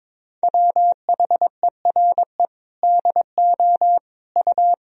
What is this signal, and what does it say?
Morse code, heard as a keyed tone, about 21 wpm, 700 Hz. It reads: WHERE DO U
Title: Sentences from Top 200 Words and Abbreviations 22wpm